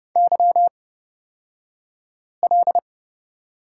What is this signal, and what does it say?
Morse code, heard as a keyed tone, about 30 wpm, spaced as for 5 wpm, 700 Hz. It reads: YL
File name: Lesson 36d - BT - Two Characters 30wpm